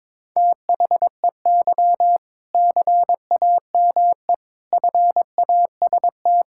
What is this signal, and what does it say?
Morse code, heard as a keyed tone, about 22 wpm, 700 Hz. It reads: THEY CAME FAST
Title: Sentences from Top 1600 Words 22wpm